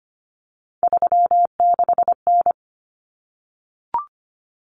Morse code, heard as a keyed tone, about 25 wpm, 700 Hz. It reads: 36N
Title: Common Cut Numbers - 3-Digit 25wpm